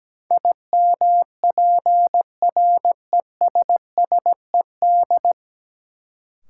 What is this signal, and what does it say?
Morse code, heard as a keyed tone, about 17 wpm, 700 Hz. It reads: IMPRESSED
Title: Sentences from Top 1600 Words - New Word Drill Alt - 17wpm